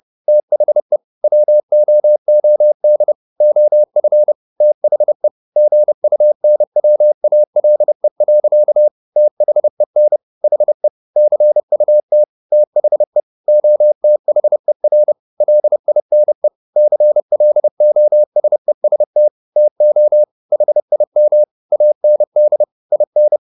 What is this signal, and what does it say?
Morse code, heard as a keyed tone, about 30 wpm, 600 Hz. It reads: THE WOOD OF THE GUNWALE. THEN HE CUT THE OTHER LINE CLOSEST TO HIM AND IN
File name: the old man and the sea / the old man and the sea 1 30wpm